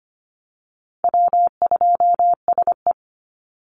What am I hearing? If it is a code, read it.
Morse code, heard as a keyed tone, about 25 wpm, 700 Hz. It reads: W2SE
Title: NA SOTA Chaser Callsigns 25wpm